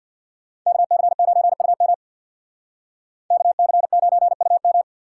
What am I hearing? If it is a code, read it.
Morse code, heard as a keyed tone, about 62 wpm, 700 Hz. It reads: KC9UM KC9UM
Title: Speed Racing - USA 2x2 Call Signs 50wpm